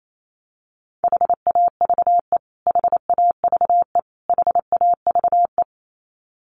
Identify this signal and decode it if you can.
Morse code, heard as a keyed tone, about 28 wpm, 700 Hz. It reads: HAVE HAVE HAVE
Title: Top 200 Words Repeated 28wpm